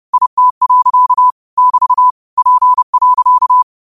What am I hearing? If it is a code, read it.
Morse code, heard as a keyed tone, about 30 wpm, 1000 Hz. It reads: IT. <BT> P.
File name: qotdfiles / 30wpm